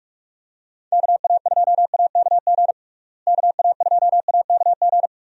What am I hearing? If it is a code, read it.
Morse code, heard as a keyed tone, about 47 wpm, 700 Hz. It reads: KA2AKG KA2AKG
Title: USA Call Signs - 6 Characters - Repeated 3x - 45wpm